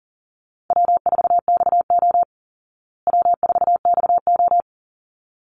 Morse code, heard as a keyed tone, about 40 wpm, 700 Hz. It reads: W4XO W4XO
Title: Easy Call Signs Repeated 2x - 40wpm